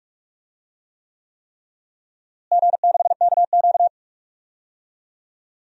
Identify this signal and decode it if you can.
Morse code, heard as a keyed tone, about 45 wpm, 700 Hz. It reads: G6KQ
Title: DXCC-100 Call Signs - 4 Characters - Repeated 3x - 45wpm